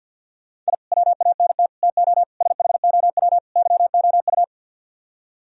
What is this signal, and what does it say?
Morse code, heard as a keyed tone, about 61 wpm, 700 Hz. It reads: I WANT TO SHOW YOU
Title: Speed-Racing - Sets of 5 Words 50wpm